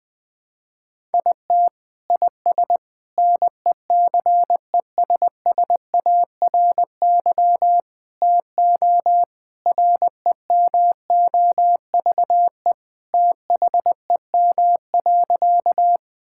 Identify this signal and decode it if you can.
Morse code, heard as a keyed tone, about 21 wpm, 700 Hz. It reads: IT IS NECESSARY TO REMOVE THEM.
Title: Sentences from Top 1000 Words - Review Part 5 - 20wpm